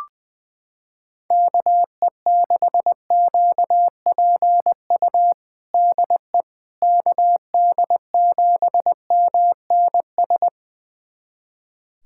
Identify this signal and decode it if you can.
Morse code, heard as a keyed tone, about 20 wpm, 700 Hz. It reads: KE6QPU DE KD7MNS
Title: USA Callsign DE Callsign 20wpm